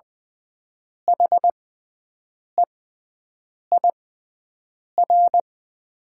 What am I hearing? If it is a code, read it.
Morse code, heard as a keyed tone, about 20 wpm, spaced as for 7 wpm, 700 Hz. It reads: HEIR
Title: 6x Character Spacing - Top 500 Words - 20wpm